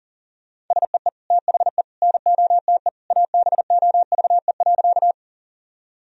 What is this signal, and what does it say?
Morse code, heard as a keyed tone, about 40 wpm, 700 Hz. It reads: SEE THE NOTE ABOVE.
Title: Sentences from Top 1000 Words - Review Part 2 - 40wpm